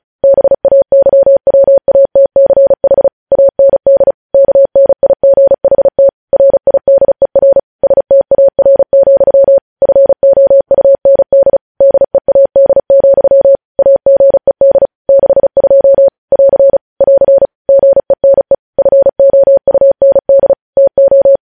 Morse code, heard as a keyed tone, about 36 wpm, 550 Hz. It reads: BAYWATCH AND KNIGHT RIDER STAR, FOUND DEAD, AGED 62 <AR> <AR> GENE FOUND TO
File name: podcast / News headlines-1x-35x35